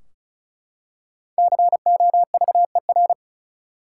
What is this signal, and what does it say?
Morse code, heard as a keyed tone, about 36 wpm, 700 Hz. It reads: COVER
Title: Five Letter Words - 35wpm